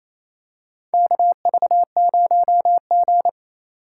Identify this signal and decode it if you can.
Morse code, heard as a keyed tone, about 28 wpm, 700 Hz. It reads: KV0G
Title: USA Call Signs - 4 Characters - 2x Character Spacing - 28wpm